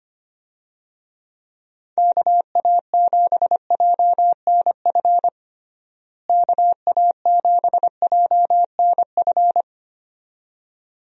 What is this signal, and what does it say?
Morse code, heard as a keyed tone, about 25 wpm, 700 Hz. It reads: KA7JNF KA7JNF
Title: USA Call Signs - 6 Characters - Repeated 2x - 25wpm